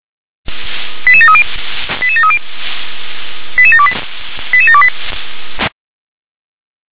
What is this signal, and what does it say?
Five-tone selective call (ZVEI), heard as quick runs of five tones, tones of about 70 ms. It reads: 88730 88730 88728 88728